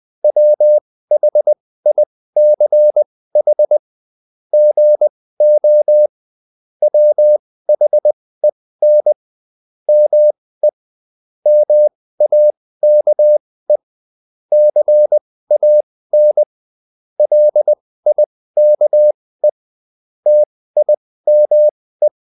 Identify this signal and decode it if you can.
Morse code, heard as a keyed tone, about 20 wpm, spaced as for 15 wpm, 600 Hz. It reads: WHICH GO WHEN ME MAKE CAN LIKE TIME
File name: Top-100-Words-15